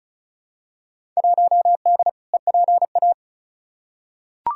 Morse code, heard as a keyed tone, about 36 wpm, 700 Hz. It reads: 1D EPA
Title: Field Day Exchanges - 35wpm